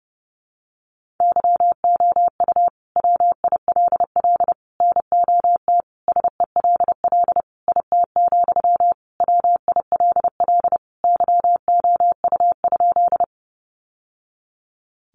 Morse code, heard as a keyed tone, about 30 wpm, 700 Hz. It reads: YOU WILL NOT SELL IT, WILL YOU?
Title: Sentences from Top 1100 Words 30wpm